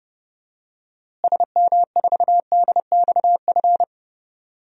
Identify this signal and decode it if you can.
Morse code, heard as a keyed tone, about 30 wpm, 700 Hz. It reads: SM4DXF
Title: DXCC-100 Call Signs - 6 Characters - Repeated 2x - 30wpm